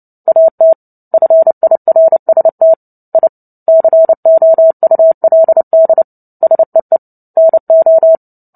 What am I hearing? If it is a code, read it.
Morse code, heard as a keyed tone, about 30 wpm, 650 Hz. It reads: AT FIRST I COULD SEE NO
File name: Book Two / 03-Chapter 3 - 29 WPM